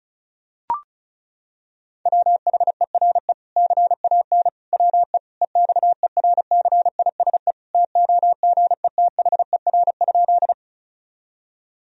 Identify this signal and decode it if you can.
Morse code, heard as a keyed tone, about 36 wpm, 700 Hz. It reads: WHERE CAN WE EXERCISE TOGETHER?
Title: Sentences from Top 1000 Words - Review Part 4 - 35wpm